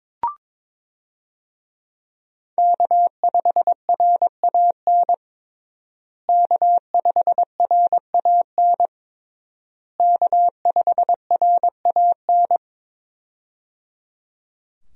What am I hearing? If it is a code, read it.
Morse code, heard as a keyed tone, about 22 wpm, 700 Hz. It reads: K5RAN K5RAN K5RAN
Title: USA Call Signs - 5 Characters - Repeated 3x - 22wpm